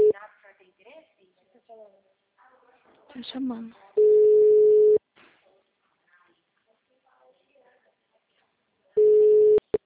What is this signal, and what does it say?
Ringback tone (Europe, 425 Hz), with speech.